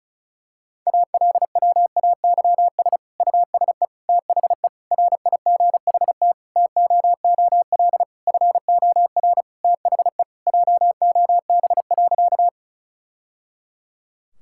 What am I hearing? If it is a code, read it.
Morse code, heard as a keyed tone, about 36 wpm, 700 Hz. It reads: ALWAYS USE THE RIGHT TOOL FOR THE JOB.
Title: Sentences from Top 1000 Words - Review Part 6 - 35wpm